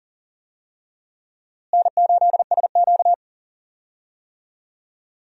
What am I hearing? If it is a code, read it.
Morse code, heard as a keyed tone, about 40 wpm, 700 Hz. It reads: N8SQ
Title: USA Call Signs - 4 Characters - Repeated 2x - 40wpm